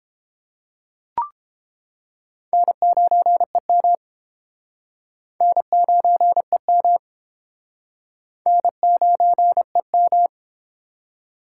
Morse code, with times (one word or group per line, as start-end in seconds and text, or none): 2.5-4.0 N9EM
5.4-7.0 N9EM
8.5-10.3 N9EM